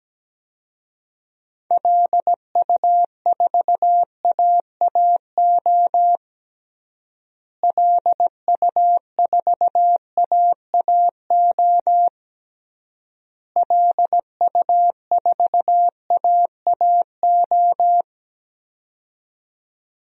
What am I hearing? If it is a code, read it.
Morse code, heard as a keyed tone, about 17 wpm, 700 Hz. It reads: LU4AAO LU4AAO LU4AAO
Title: DXCC-100 Call Signs - 6 Characters - Repeated 3x - 17wpm